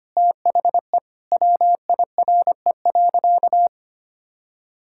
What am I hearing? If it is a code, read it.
Morse code, heard as a keyed tone, about 25 wpm, 700 Hz. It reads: THE WIRE.
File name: practice / 25wpm / Sentences with 3-Word Phrases - 25wpm